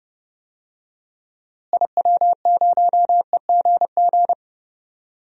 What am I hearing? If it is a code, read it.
Morse code, heard as a keyed tone, about 32 wpm, 700 Hz. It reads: IW0EGG
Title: DXCC-100 Call Signs - 6 Characters - Repeated 2x - 30wpm